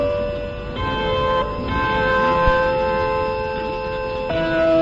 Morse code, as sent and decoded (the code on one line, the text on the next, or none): none